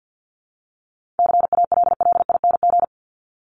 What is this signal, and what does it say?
Morse code, heard as a keyed tone, about 50 wpm, 700 Hz. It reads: CALLING
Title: ING Suffix - Top 100 Words with Base Words 50wpm